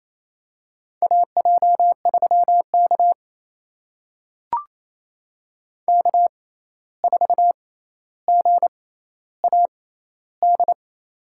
Morse code, heard as a keyed tone, about 28 wpm, spaced as for 13 wpm, 700 Hz. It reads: AJ3K K4GAD
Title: USA Call Signs - 6x Character Spacing - 28wpm